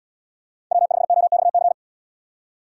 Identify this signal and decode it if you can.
Morse code, heard as a keyed tone, about 77 wpm, 700 Hz. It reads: W5OXC